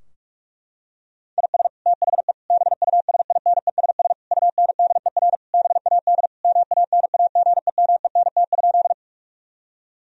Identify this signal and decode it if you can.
Morse code, heard as a keyed tone, about 42 wpm, 700 Hz. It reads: IS THE BUSINESS UNDER BAD MANAGEMENT?
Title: Sentences from Top 1000 Words - Review Part 4 - 45wpm